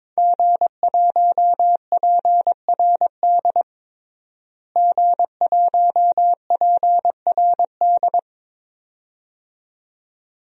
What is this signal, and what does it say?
Morse code, heard as a keyed tone, about 22 wpm, 700 Hz. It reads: G1PRD G1PRD
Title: DXCC-100 Call Signs - 5 Characters - Repeated 2x - 22wpm